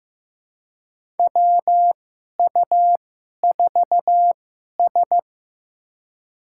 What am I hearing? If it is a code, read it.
Morse code, heard as a keyed tone, about 15 wpm, spaced as for 11 wpm, 700 Hz. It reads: WU4S
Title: USA Call Signs - 4 Characters - 2x Character Spacing - 15wpm